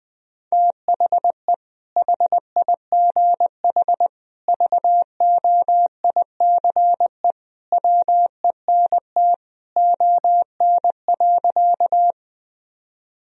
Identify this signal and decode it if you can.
Morse code, heard as a keyed tone, about 20 wpm, 700 Hz. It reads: THE HIGH VOICE WENT ON.
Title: Sentences from Top 1000 Words - Review Part 1 - 20wpm